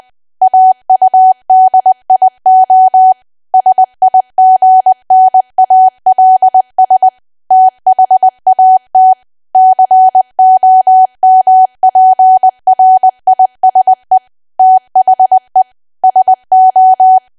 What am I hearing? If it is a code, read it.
Morse code, heard as a keyed tone, about 20 wpm, 750 Hz. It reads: AUDIO SIGNALS THAT COMPRISE THE SO
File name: cwa / W1AW-Level3 / 20WPM